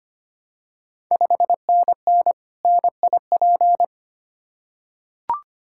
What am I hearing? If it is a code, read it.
Morse code, heard as a keyed tone, about 25 wpm, 700 Hz. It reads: 5NN NIP
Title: Ontario QSO Party Exchange 25wpm